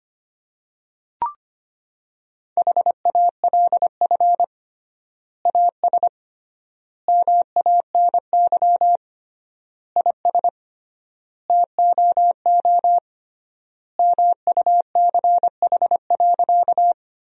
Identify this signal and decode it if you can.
Morse code, heard as a keyed tone, about 25 wpm, 700 Hz. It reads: HALF AS MANY IS TOO MUCH.
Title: 3x Word-Spacing - Sentences from Top 400 Words - 25wpm